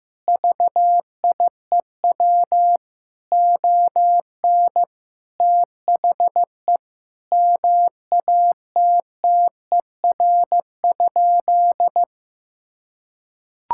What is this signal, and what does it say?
Morse code, heard as a keyed tone, about 15 wpm, 700 Hz. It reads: VIEW ON THE MATTER?
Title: Sentences from Top 1000 Words - Review Part 6 - 15wpm